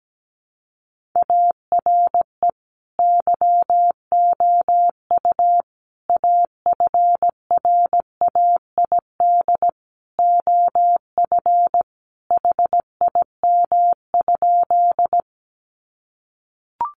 Morse code, heard as a keyed tone, about 17 wpm, 700 Hz. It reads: ARE YOU AFRAID OF HIM?
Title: Sentences from Top 1500 Words 17wpm